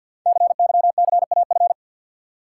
Morse code, heard as a keyed tone, about 50 wpm, 700 Hz. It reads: CYCAF
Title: Code Groups - 5 Letters - 50wpm